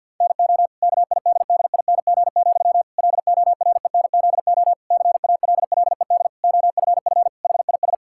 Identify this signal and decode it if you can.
Morse code, heard as a keyed tone, about 50 wpm, 700 Hz. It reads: NO KIDDING, LORENZO CALLED OFF HIS
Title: Pangram Challenge v2 - 50wpm